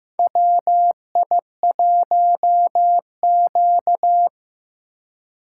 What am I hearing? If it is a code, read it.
Morse code, heard as a keyed tone, about 15 wpm, 700 Hz. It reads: WI1Q
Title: USA Call Signs - 3x Character Spacing - 15wpm